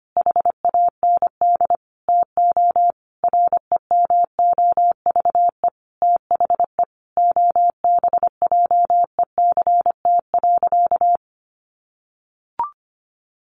Morse code, heard as a keyed tone, about 25 wpm, 700 Hz. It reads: HAND TO REMOVE THE OBJECT.
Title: Sentences from Top 800 Words 25wpm